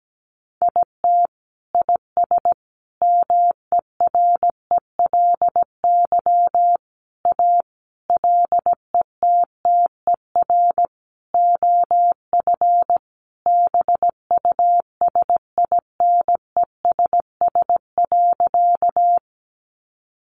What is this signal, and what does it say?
Morse code, heard as a keyed tone, about 17 wpm, 700 Hz. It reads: IT IS MERELY A LETTER OF BUSINESS.
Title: Sentences from Top 1400 Words 17wpm